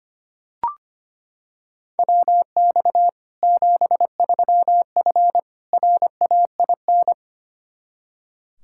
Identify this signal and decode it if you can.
Morse code, heard as a keyed tone, about 25 wpm, 700 Hz. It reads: WX 73F RAIN
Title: Practice QSO - Exchange 2 - Info - 25wpm